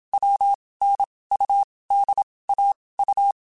Morse code, heard as a keyed tone, about 27 wpm, 800 Hz. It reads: W N U D A U
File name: Basic Sets / Basic Letter Groups Collective7 / AWUNDG@25